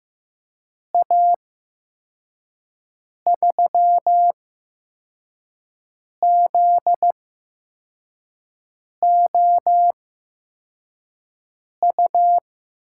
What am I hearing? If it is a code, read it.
Morse code, heard as a keyed tone, about 15 wpm, spaced as for 4 wpm, 700 Hz. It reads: A 3ZOU
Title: USA Call Signs - 6 Characters - 8x Character Spacing - 15wpm